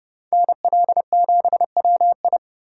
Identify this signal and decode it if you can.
Morse code, heard as a keyed tone, about 27 wpm, 700 Hz. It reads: NL7WI